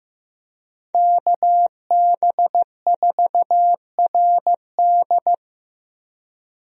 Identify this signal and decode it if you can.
Morse code, heard as a keyed tone, about 14 wpm, 700 Hz. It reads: KB4RD
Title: USA Call Signs - 5 Characters - Repeated 2x - 15wpm